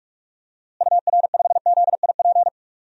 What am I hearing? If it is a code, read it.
Morse code, heard as a keyed tone, about 45 wpm, 700 Hz. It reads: UR5ZIP
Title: DXCC-100 Call Signs - 6 Characters - Repeated 3x - 45wpm